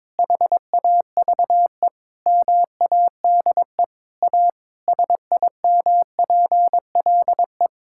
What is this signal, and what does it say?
Morse code, heard as a keyed tone, about 22 wpm, 700 Hz. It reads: HAVE MADE A SIMPLE
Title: Sentences from Top 1000 Words - Review Part 6 - 22wpm